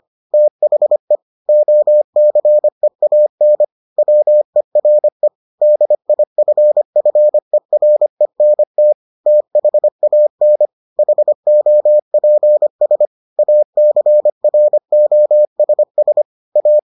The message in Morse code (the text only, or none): THE OCEAN WERE DIFFERENT THAN HOPS ACROSS A